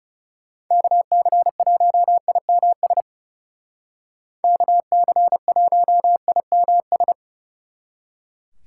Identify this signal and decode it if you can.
Morse code, heard as a keyed tone, about 38 wpm, 700 Hz. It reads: KC1IMS KC1IMS